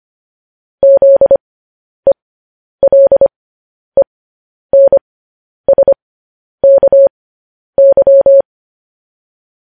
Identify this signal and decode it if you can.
Morse code, heard as a keyed tone, about 25 wpm, spaced as for 10 wpm, 550 Hz. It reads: ZELENSKY